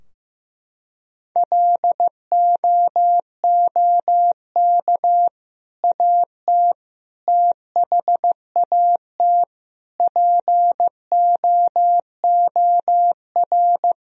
Morse code, heard as a keyed tone, about 15 wpm, 700 Hz. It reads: LOOK AT THAT POOR
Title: Sentences from Top 1000 Words - Review Part 2 - 15wpm